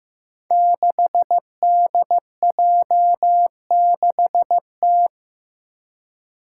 Morse code, heard as a keyed tone, about 15 wpm, 700 Hz. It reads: 6DJ6T
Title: Five Challenging Letter-Number Combinations 15wpm